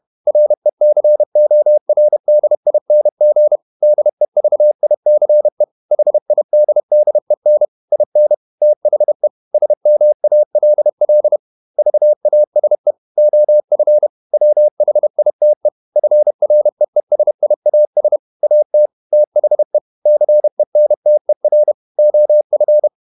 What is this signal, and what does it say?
Morse code, heard as a keyed tone, about 31 wpm, 600 Hz. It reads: RECORDING DEVICE HIDDEN IN THE SMALL VASE OF WHITE FREESIAS AT THE CENTER OF